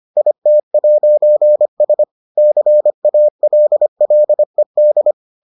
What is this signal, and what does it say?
Morse code, heard as a keyed tone, about 25 wpm, 600 Hz. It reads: IT'S CALLED